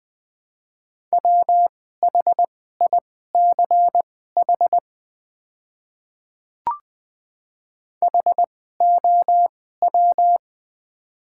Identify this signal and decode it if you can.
Morse code, heard as a keyed tone, about 20 wpm, spaced as for 14 wpm, 700 Hz. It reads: WHICH HOW
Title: 2x Character Spacing - Top 100 Words - 20wpm